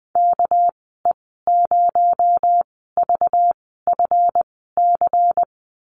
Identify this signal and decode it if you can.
Morse code, heard as a keyed tone, about 20 wpm, spaced as for 14 wpm, 700 Hz. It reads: KE0VFC
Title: 219 Lesson 35g - Number 0 - Call Signs 20wpm